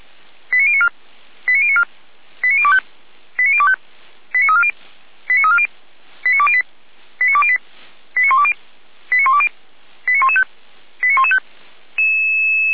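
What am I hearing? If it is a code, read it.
Five-tone selective call (ZVEI), heard as quick runs of five tones, tones of about 70 ms. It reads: 89094 89094 89025 89025 89340 89340 89208 89208 89120 89120 89115 89115
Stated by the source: Received by SDR, ZVEI five-tone call.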